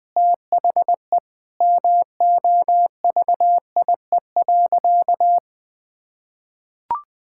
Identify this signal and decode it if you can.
Morse code, heard as a keyed tone, about 20 wpm, 700 Hz. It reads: THE MOVIE.
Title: Sentences with 2-Word Phrases - 20wpm